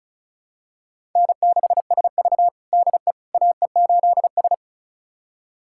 Morse code, heard as a keyed tone, about 36 wpm, 700 Hz. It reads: N6SV DE AE8S
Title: USA Callsign DE Callsign - 4 Characters - 35wpm